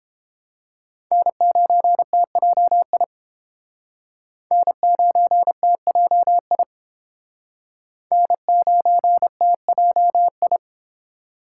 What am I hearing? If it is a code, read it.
Morse code, heard as a keyed tone, about 30 wpm, 700 Hz. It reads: N9TJI N9TJI N9TJI